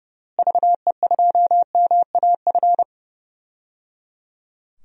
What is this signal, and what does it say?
Morse code, heard as a keyed tone, about 30 wpm, 700 Hz. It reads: VE2MAF
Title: DXCC-100 Call Signs - 6 Characters - Repeated 3x - 30wpm